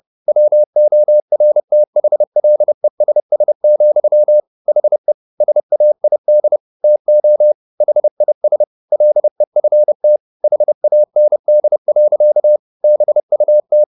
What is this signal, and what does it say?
Morse code, heard as a keyed tone, about 30 wpm, 600 Hz. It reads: WORTHLESS, HE SAID TO HIS LEFT HAND. BUT